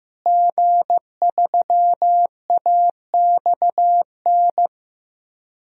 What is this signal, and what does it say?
Morse code, heard as a keyed tone, about 15 wpm, 700 Hz. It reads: G3AXN
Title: DXCC-100 Call Signs - 5 Characters - Repeated 3x - 15wpm